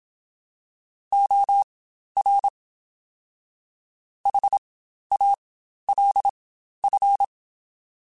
Morse code, heard as a keyed tone, about 27 wpm, spaced as for 12 wpm, 800 Hz. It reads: OR HALF
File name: Spaced / Chuckles Spaced-mp3s / HalfFull sp @25